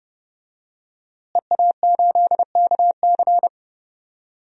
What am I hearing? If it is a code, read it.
Morse code, heard as a keyed tone, about 30 wpm, 700 Hz. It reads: EA8KC
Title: DXCC-100 Call Signs - 5 Characters - Repeated 2x - 30wpm